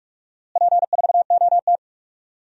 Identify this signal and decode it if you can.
Morse code, heard as a keyed tone, about 44 wpm, 700 Hz. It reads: P4OT
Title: DXCC-100 Call Signs - 4 Characters - Repeated 2x - 45wpm